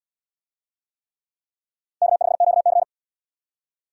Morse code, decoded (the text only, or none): K5QC